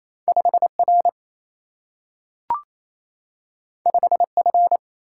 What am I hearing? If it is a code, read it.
Morse code, heard as a keyed tone, about 28 wpm, 700 Hz. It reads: 5 R 5 F
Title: Single-Letter Single-Number 28wpm